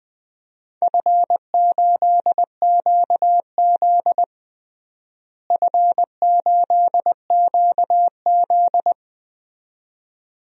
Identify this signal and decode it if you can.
Morse code, heard as a keyed tone, about 21 wpm, 700 Hz. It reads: F8QZ F8QZ
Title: DXCC-100 Call Signs - 4 Characters - Repeated 2x - 20wpm